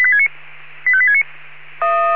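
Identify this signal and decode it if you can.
Five-tone selective call (ZVEI), heard as quick runs of five tones, tones of about 70 ms. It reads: 85869
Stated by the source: Five-tone ZVEI selective call 85369, received on SDR.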